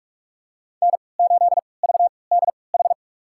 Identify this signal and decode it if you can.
Morse code, heard as a keyed tone, about 46 wpm, spaced as for 26 wpm, 700 Hz. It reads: N8VDH
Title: USA Call Signs - 5 Characters - 3x Character Spacing - 45wpm